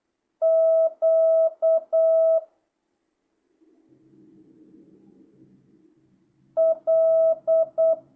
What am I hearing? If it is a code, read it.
Morse code, heard as a keyed tone, about 8 wpm, 650 Hz. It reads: Q L